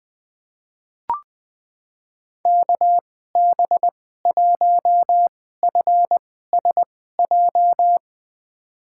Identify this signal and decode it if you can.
Morse code, heard as a keyed tone, about 20 wpm, spaced as for 15 wpm, 700 Hz. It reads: KB1FSJ
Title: USA Call Signs - 2x Character Spacing - 20wpm